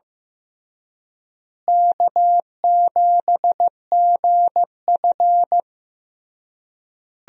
Morse code, heard as a keyed tone, about 15 wpm, 700 Hz. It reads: K7GF
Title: USA Call Signs - 4 Characters - Repeated 3x - 15wpm